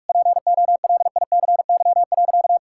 Morse code, heard as a keyed tone, about 45 wpm, 700 Hz. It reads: POLICY.
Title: Sentences from Top 1000 Words - Review Part 5 - 45wpm